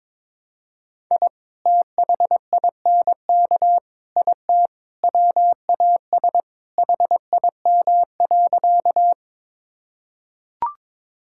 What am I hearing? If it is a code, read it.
Morse code, heard as a keyed tone, about 22 wpm, 700 Hz. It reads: I THINK IT WAS HIM.
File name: practice / 22wpm / Sentences with 4-Word Phrases - 22wpm